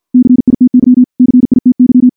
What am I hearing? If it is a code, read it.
Morse code, heard as a keyed tone, about 49 wpm, 250 Hz. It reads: CITY CITY
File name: morse / morse-training-50WPM